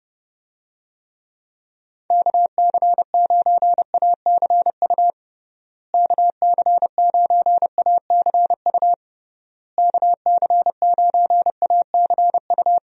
Morse code, heard as a keyed tone, about 30 wpm, 700 Hz. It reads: KC9ACU KC9ACU KC9ACU
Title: USA Call Signs - 6 Characters - Repeated 3x - 30wpm